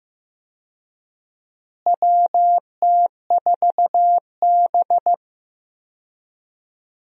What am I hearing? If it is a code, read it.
Morse code, heard as a keyed tone, about 15 wpm, 700 Hz. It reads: WT4B
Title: USA Call Signs - 4 Characters - Repeated 3x - 15wpm